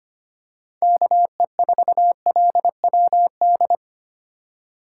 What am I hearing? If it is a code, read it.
Morse code, heard as a keyed tone, about 25 wpm, 700 Hz. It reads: KE4LWD
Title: USA Call Signs - 6 Characters - 2x Character Spacing - 25wpm